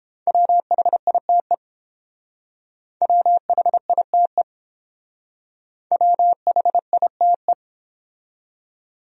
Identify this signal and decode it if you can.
Morse code, heard as a keyed tone, about 28 wpm, 700 Hz. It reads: WHITE WHITE WHITE